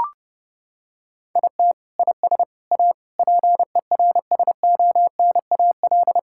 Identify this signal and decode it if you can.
Morse code, heard as a keyed tone, about 30 wpm, 700 Hz. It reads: IT IS A PERSONAL
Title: Sentences from Top 1000 Words - Review Part 5 - 30wpm